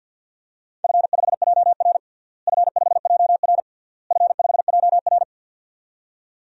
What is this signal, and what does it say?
Morse code, heard as a keyed tone, about 53 wpm, 700 Hz. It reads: F5JR F5JR F5JR
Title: DXCC-100 Call Signs - 4 Characters - Repeated 3x - 50wpm